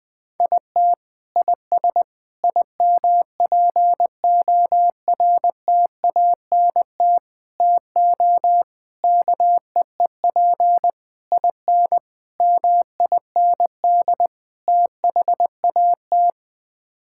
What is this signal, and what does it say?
Morse code, heard as a keyed tone, about 20 wpm, 700 Hz. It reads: IT IS IMPORTANT TO KEEP IN MIND THAT
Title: Sets of 8 Words - 20wpm